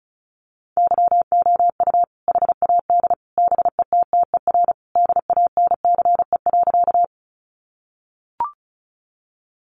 Morse code, heard as a keyed tone, about 34 wpm, 700 Hz. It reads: YOU HAD BETTER DANCE.
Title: Sentences from Top 700 Words 35wpm